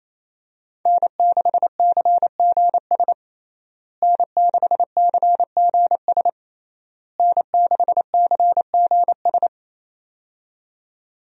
Morse code, heard as a keyed tone, about 28 wpm, 700 Hz. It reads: N6CGS N6CGS N6CGS
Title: USA Call Signs - 5 Characters - Repeated 3x - 28wpm